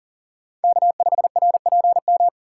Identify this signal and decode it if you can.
Morse code, heard as a keyed tone, about 40 wpm, 700 Hz. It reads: K5RPM
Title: USA Call Signs - 5 Characters - 4x Character Spacing - 40wpm